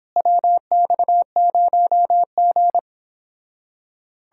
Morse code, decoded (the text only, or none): WX0G